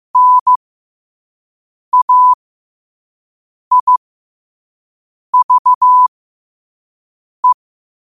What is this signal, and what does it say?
Morse code, heard as a keyed tone, about 15 wpm, spaced as for 5 wpm, 1000 Hz. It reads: NAIVE
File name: qotdfiles / 5wpm